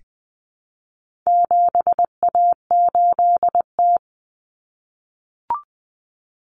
Morse code, heard as a keyed tone, about 20 wpm, 700 Hz. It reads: 7A8T
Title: Common Cut Numbers - 4-Digit 20wpm